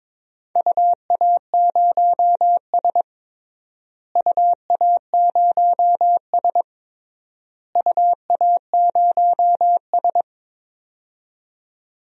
Morse code, heard as a keyed tone, about 21 wpm, 700 Hz. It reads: UA0S UA0S UA0S
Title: DXCC-100 Call Signs - 4 Characters - Repeated 3x - 22wpm